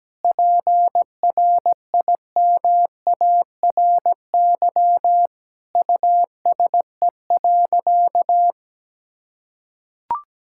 Morse code, heard as a keyed tone, about 17 wpm, 700 Hz. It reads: PRIMARY USE.
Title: Sentences from Top 1000 Words - Review Part 5 - 17wpm